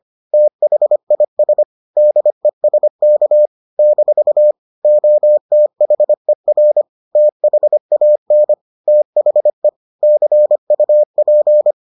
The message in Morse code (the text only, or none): THIS DESK - OTHER THAN THE CUP